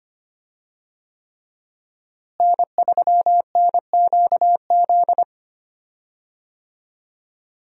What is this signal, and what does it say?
Morse code, heard as a keyed tone, about 25 wpm, 700 Hz. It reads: N3NQZ
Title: USA Call Signs - 5 Characters - Repeated 3x - 25wpm